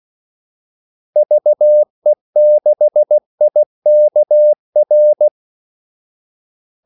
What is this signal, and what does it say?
Morse code, heard as a keyed tone, about 16 wpm, 600 Hz. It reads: VE6IKR
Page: Kaldesignaler 16 wpm